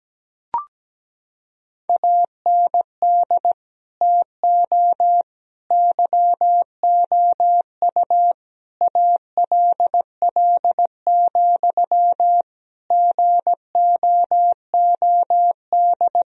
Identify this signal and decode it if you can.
Morse code, heard as a keyed tone, about 17 wpm, 700 Hz. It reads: AND TO YOU ALL, GOOD
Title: Sentences from Top 1000 Words - Review Part 3 - 17wpm